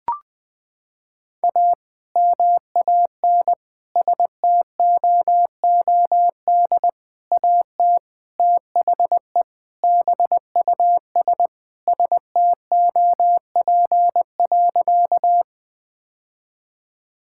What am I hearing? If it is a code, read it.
Morse code, heard as a keyed tone, about 21 wpm, 700 Hz. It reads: A MAN STOOD AT THE BUS STOP.
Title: Sentences with 2-Word Phrases - 20wpm